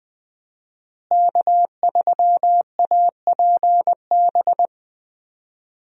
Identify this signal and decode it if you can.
Morse code, heard as a keyed tone, about 20 wpm, 700 Hz. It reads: K3APB
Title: USA Call Signs - 5 Characters - Repeated 2x - 20wpm